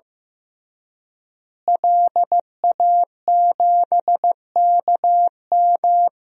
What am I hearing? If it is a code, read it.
Morse code, heard as a keyed tone, about 15 wpm, 700 Hz. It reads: LA7KM
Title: DXCC-100 Call Signs - 5 Characters - Repeated 2x - 15wpm